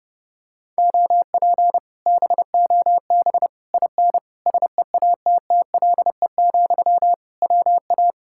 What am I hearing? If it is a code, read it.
Morse code, heard as a keyed tone, about 28 wpm, 700 Hz. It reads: OP BOB IN SEATTLE, WA